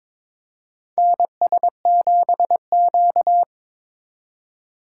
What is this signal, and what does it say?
Morse code, heard as a keyed tone, about 23 wpm, 700 Hz. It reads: NS7Q
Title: USA Call Signs - 4 Characters - 4x Character Spacing - 22wpm